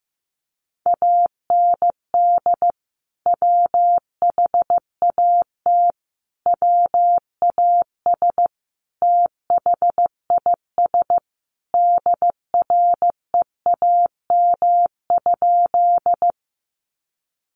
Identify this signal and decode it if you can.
Morse code, heard as a keyed tone, about 15 wpm, 700 Hz. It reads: AND WHAT WAS THIS DREAM?
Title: Sentences from Top 1500 Words 15wpm